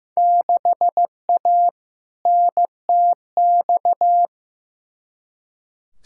Morse code, heard as a keyed tone, about 15 wpm, 700 Hz. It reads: 6A NTX
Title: Field Day Exchanges - 15wpm